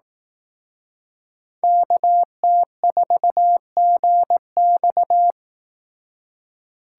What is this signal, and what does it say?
Morse code, heard as a keyed tone, about 18 wpm, 700 Hz. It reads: KT4GX
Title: Speed Racing - USA 2x2 Call Signs 15wpm